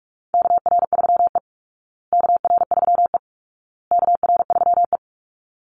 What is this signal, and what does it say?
Morse code, heard as a keyed tone, about 44 wpm, 700 Hz. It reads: KR3E KR3E KR3E
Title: Call Signs Repeated 2x - 45wpm